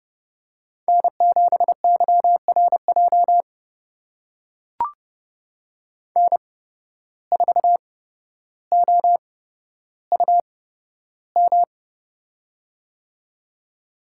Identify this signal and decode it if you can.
Morse code, heard as a keyed tone, about 30 wpm, spaced as for 13 wpm, 700 Hz. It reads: N7YRJ N4OUM
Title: USA Call Signs - 5 Characters - 8x Character Spacing - 30wpm